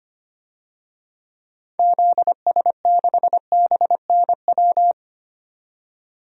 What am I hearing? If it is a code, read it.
Morse code, heard as a keyed tone, about 25 wpm, 700 Hz. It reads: ZS6BNW
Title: DXCC-100 Call Signs - 6 Characters - 25wpm